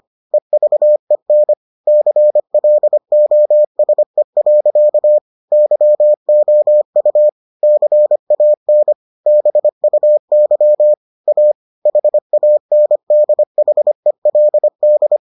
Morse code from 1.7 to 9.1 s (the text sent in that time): CLOSE. YOU CAN